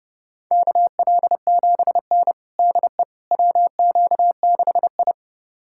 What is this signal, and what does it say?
Morse code, heard as a keyed tone, about 32 wpm, 700 Hz. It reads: KL7N DE WQ6I
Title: USA Callsign DE Callsign - 4 Characters - 30wpm